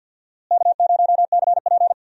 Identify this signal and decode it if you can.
Morse code, heard as a keyed tone, about 46 wpm, 700 Hz. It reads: K0CP
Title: USA Call Signs - 4 Characters - Repeated 2x - 50wpm